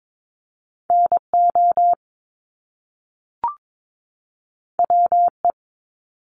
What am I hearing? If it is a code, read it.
Morse code, heard as a keyed tone, about 22 wpm, 700 Hz. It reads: NO WE
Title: Top Two Letter Combinations in Words 22wpm